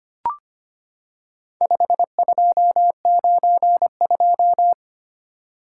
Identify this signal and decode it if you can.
Morse code, heard as a keyed tone, about 25 wpm, 700 Hz. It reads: 5292
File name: practice / 25wpm / Four Letter-Number Combinations - 25wpm